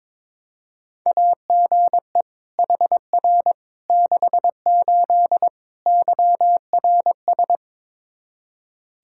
Morse code, heard as a keyed tone, about 22 wpm, 700 Hz. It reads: AGE HR 68 YRS
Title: Practice QSO - Exchange 3 - More Info - 22wpm